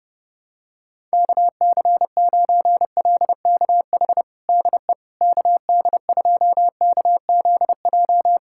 Morse code, heard as a keyed tone, about 30 wpm, 700 Hz. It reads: KC9LKH DE KD2KZJ
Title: USA Callsign DE Callsign 30wpm